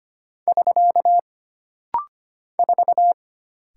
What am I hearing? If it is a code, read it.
Morse code, heard as a keyed tone, about 25 wpm, 700 Hz. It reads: <SK> 4
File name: Single Character (Letter:Number:Punctuation:Prosign) - Rapid-Fire - 25wpm